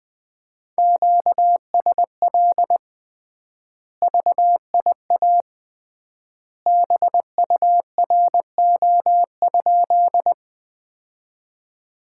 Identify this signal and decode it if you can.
Morse code, heard as a keyed tone, about 20 wpm, 700 Hz. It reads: QSL VIA BURO?
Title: Common QSO Phrases - 3x Word-Spacing - 20wpm